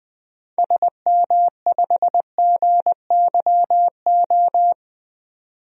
Morse code, heard as a keyed tone, about 21 wpm, 700 Hz. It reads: SM5GYO
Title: DXCC-100 Call Signs - 6 Characters - 20wpm